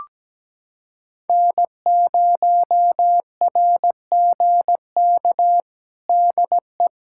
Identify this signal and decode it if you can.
Morse code, heard as a keyed tone, about 17 wpm, 700 Hz. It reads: N0RGK DE
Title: USA Callsign DE Callsign - 5 Characters - 17wpm